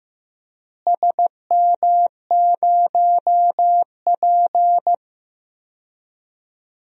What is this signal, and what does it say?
Morse code, heard as a keyed tone, about 15 wpm, 700 Hz. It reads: SM0P
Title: DXCC-100 Call Signs - 4 Characters - Repeated 2x - 15wpm